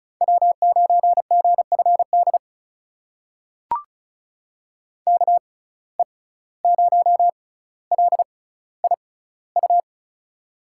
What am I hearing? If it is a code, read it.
Morse code, heard as a keyed tone, about 36 wpm, spaced as for 17 wpm, 700 Hz. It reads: W9GFD KE0LIU
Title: USA Call Signs - 6x Character Spacing - 35wpm